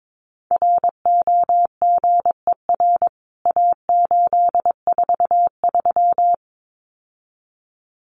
Morse code, heard as a keyed tone, about 22 wpm, 700 Hz. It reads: ROGER A843
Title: CWT Exchange 22wpm